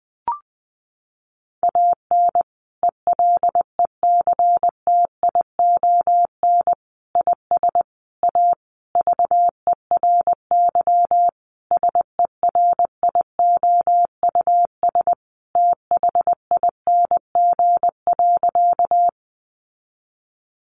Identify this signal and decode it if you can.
Morse code, heard as a keyed tone, about 19 wpm, 700 Hz. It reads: AN ELECTION IS A VERY SERIOUS THING.
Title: Sentences from Top 1500 Words 20wpm